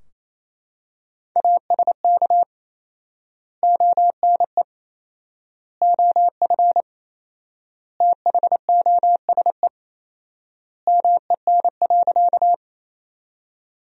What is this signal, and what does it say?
Morse code, heard as a keyed tone, about 28 wpm, 700 Hz. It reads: ASK ONE OF THOSE MEN.
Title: 4x Word-Spacing - Sentences from Top 300 Words - 28wpm